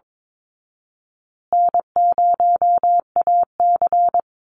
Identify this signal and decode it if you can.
Morse code, heard as a keyed tone, about 22 wpm, 700 Hz. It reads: N0AC
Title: Easy Call Signs Repeated 3x - 22wpm